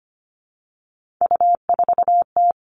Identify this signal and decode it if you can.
Morse code, heard as a keyed tone, about 25 wpm, 700 Hz. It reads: U4T
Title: All Cut Numbers - 3-Digit 25wpm